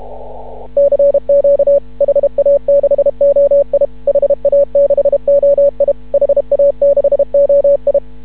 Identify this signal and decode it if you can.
Morse code, heard as a keyed tone, about 31 wpm, 560 Hz. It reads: CQ HA6OI HA6OI HA6OI
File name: CW / ha6oi 2